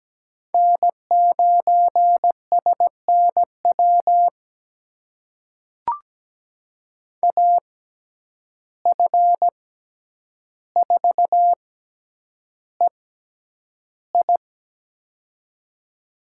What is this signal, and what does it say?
Morse code, heard as a keyed tone, about 17 wpm, spaced as for 9 wpm, 700 Hz. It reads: N9SNW AF4EI
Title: USA Call Signs - 5 Characters - 6x Character Spacing - 17wpm